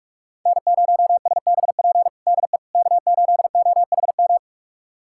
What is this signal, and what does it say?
Morse code, heard as a keyed tone, about 46 wpm, 700 Hz. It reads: N0SBP DE K8OHM
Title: USA Callsign DE Callsign - 5 Characters - 45wpm